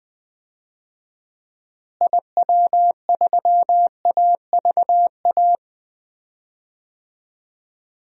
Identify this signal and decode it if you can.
Morse code, heard as a keyed tone, about 20 wpm, 700 Hz. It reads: IW3AVA
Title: DXCC-100 Call Signs - 6 Characters - Repeated 2x - 20wpm